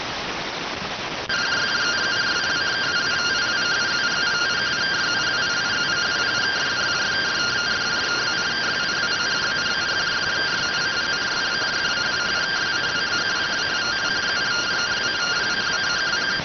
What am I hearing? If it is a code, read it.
Two-tone teleprinter signal, about 170 Hz shift.
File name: findings / MR AXIAL 33,99mm 1.gen-44 rx